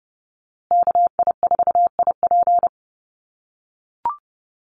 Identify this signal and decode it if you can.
Morse code, heard as a keyed tone, about 30 wpm, 700 Hz. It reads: KI4IP
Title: USA Call Signs Repeated 3x 30wpm